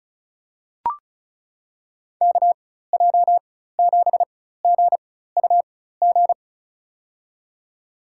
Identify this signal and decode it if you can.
Morse code, heard as a keyed tone, about 36 wpm, spaced as for 16 wpm, 700 Hz. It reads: KJ7GUG
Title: USA Call Signs - 6 Characters - 4x Character Spacing - 35wpm